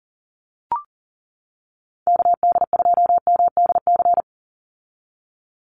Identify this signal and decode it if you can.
Morse code, heard as a keyed tone, about 40 wpm, 700 Hz. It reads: KD2MDC
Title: USA Call Signs 40wpm